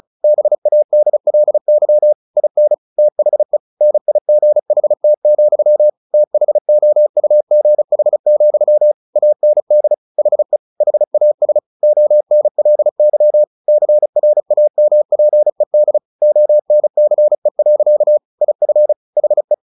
Morse code, heard as a keyed tone, about 36 wpm, 600 Hz. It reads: BADLY IN THE NIGHT, THOUGH, AND HE HAS ONLY CRAMPED ONCE. IF HE
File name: the old man and the sea 2 35wpm